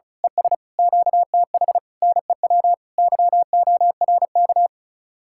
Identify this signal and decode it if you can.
Morse code, heard as a keyed tone, about 33 wpm, 700 Hz. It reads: ES QTH NEW YORK
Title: Speed-Racing Common QSO Phrases - 35wpm